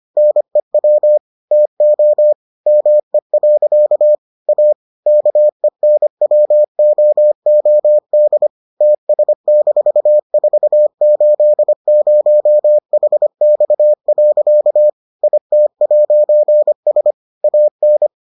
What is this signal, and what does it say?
Morse code, heard as a keyed tone, about 25 wpm, 600 Hz. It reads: NEW TO ME. A KENWOOD TS-480HX. IT'S AN